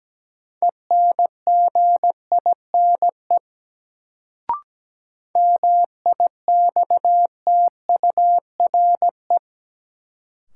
Morse code, heard as a keyed tone, about 17 wpm, 700 Hz. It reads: ENGINE MIXTURE
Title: Sentences from Top 1500 Words - New Word Drill - 17wpm